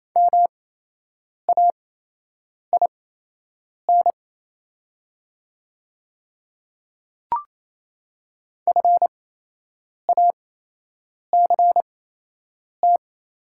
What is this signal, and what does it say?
Morse code, heard as a keyed tone, about 28 wpm, spaced as for 8 wpm, 700 Hz. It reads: MAIN FACT
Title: 8x Character Spacing - Top 500 Words - 28wpm